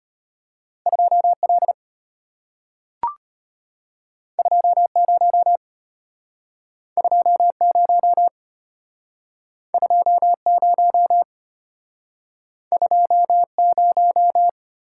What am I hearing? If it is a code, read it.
Morse code, heard as a keyed tone, about 33 wpm, 700 Hz. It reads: 2L 20 20 20 20